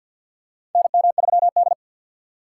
Morse code, heard as a keyed tone, about 50 wpm, 700 Hz. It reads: NM3D
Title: USA Call Signs - 4x Character Spacing - 50wpm